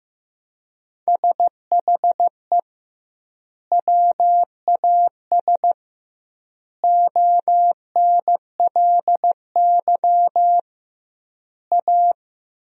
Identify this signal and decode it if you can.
Morse code, heard as a keyed tone, about 16 wpm, 700 Hz. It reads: SHE WAS ONLY A
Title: 2x Word-Spacing - Sentences from Top 400 Words - 15wpm